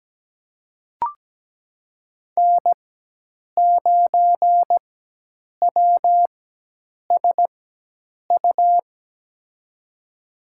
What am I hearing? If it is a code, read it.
Morse code, heard as a keyed tone, about 17 wpm, spaced as for 8 wpm, 700 Hz. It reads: N9WSU
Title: USA Call Signs - 5 Characters - 4x Character Spacing - 17wpm